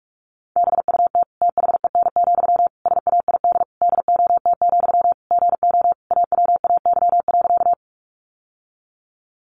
Morse code, heard as a keyed tone, about 44 wpm, 700 Hz. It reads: BUT THEN, SAID DOT, GO AWAY.
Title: Sentences from Top 1500 Words 45wpm